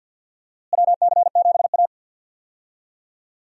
Morse code, heard as a keyed tone, about 52 wpm, 700 Hz. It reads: WC7A
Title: USA Call Signs - 4 Characters - Repeated 2x - 50wpm